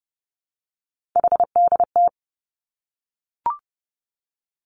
Morse code, heard as a keyed tone, about 31 wpm, 700 Hz. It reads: HDT
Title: Three Challenging Letter-Number Combinations 30wpm